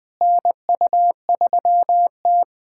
Morse code, heard as a keyed tone, about 21 wpm, 700 Hz. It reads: NU3T
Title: USA Callsign DE Callsign - 4 Characters - 20wpm